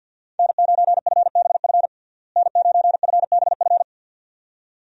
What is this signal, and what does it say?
Morse code, heard as a keyed tone, about 51 wpm, 700 Hz. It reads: N9FBF N9FBF
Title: USA Call Signs - 5 Characters - Repeated 2x - 50wpm